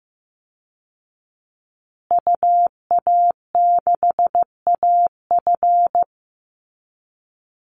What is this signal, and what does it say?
Morse code, heard as a keyed tone, about 15 wpm, 700 Hz. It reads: UA6AF
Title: Call Signs Repeated 2x - 15wpm